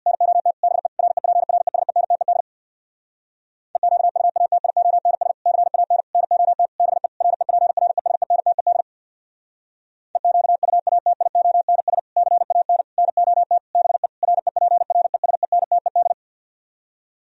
Morse code, heard as a keyed tone, about 64 wpm, 700 Hz. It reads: NOT BE REPRESENTED EQUATIONS CAN NOT BE REPRESENTED EQUATIONS CAN NOT BE REPRESENTED